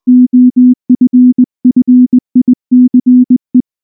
Morse code, heard as a keyed tone, about 19 wpm, 250 Hz. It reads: OFFICE
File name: morse / morse-training-20WPM-7